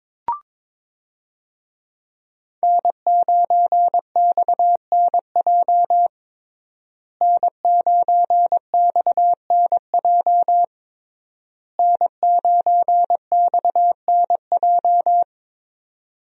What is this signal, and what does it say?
Morse code, heard as a keyed tone, about 22 wpm, 700 Hz. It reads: N9XNJ N9XNJ N9XNJ
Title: USA Call Signs - 5 Characters - Repeated 3x - 22wpm